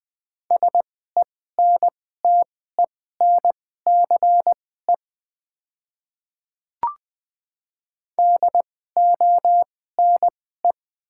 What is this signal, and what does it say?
Morse code, heard as a keyed tone, about 20 wpm, spaced as for 14 wpm, 690 Hz. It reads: SENTENCE DONE
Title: 2x Character Spacing - Top 500 Words - 20wpm